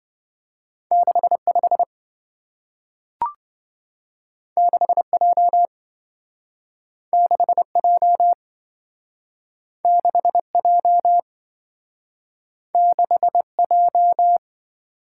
Morse code, heard as a keyed tone, about 27 wpm, 700 Hz. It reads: 65 6J 6J 6J 6J